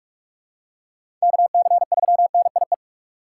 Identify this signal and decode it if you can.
Morse code, heard as a keyed tone, about 42 wpm, 700 Hz. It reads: KC3NIE
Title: K1USN SST QSO Practice - 45wpm